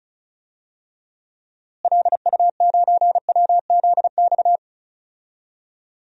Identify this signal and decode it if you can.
Morse code, heard as a keyed tone, about 34 wpm, 700 Hz. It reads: LU9WZX
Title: DXCC-100 Call Signs - 6 Characters - Repeated 3x - 35wpm